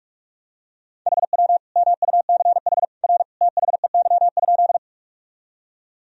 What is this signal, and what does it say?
Morse code, heard as a keyed tone, about 41 wpm, 700 Hz. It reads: HW MUCH R THEY?
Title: Sentences from Top 200 Words and Abbreviations 45wpm